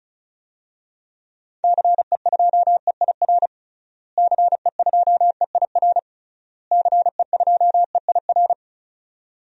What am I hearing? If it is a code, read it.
Morse code, heard as a keyed tone, about 36 wpm, 700 Hz. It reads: CE2EIR CE2EIR CE2EIR
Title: DXCC-100 Call Signs - 6 Characters - Repeated 3x - 35wpm